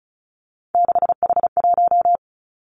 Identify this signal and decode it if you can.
Morse code, heard as a keyed tone, about 34 wpm, 700 Hz. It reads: 6H1
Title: Three Challenging Letter-Number Combinations 35wpm